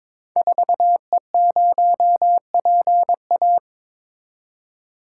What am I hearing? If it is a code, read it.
Morse code, heard as a keyed tone, about 22 wpm, 700 Hz. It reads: 4E0PA
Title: Code Groups - 5 Letter:Numbers - 22wpm